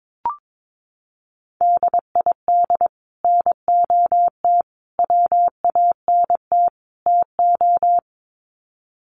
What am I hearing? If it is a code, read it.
Morse code, heard as a keyed tone, about 22 wpm, 700 Hz. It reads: DID NOT WANT TO
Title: Sets of 4 Words 22wpm